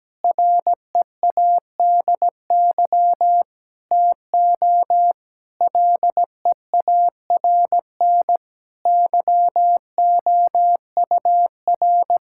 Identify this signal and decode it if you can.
Morse code, heard as a keyed tone, about 17 wpm, 700 Hz. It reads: READY TO LEARN YOUR
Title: Sentences from Top 1000 Words - Review Part 6 - 17wpm